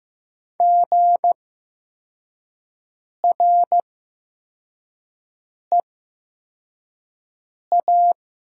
Morse code, heard as a keyed tone, about 15 wpm, spaced as for 4 wpm, 700 Hz. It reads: GREA
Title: 8x Character Spacing - Top 500 Words - 15wpm